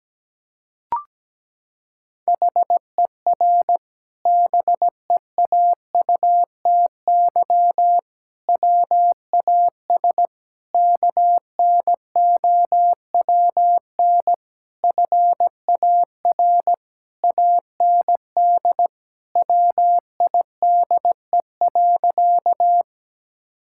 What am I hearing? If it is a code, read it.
Morse code, heard as a keyed tone, about 17 wpm, 700 Hz. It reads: HER BEAUTY WAS KNOWN FAR AND WIDE.
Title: Sentences from Top 1000 Words - Review Part 6 - 17wpm